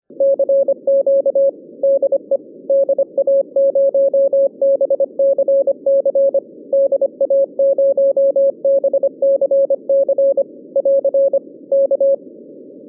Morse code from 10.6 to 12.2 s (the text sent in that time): <AR> K